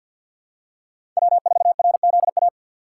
Morse code, heard as a keyed tone, about 53 wpm, 700 Hz. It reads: W4RZA
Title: USA Call Signs - 5 Characters - 3x Character Spacing - 50wpm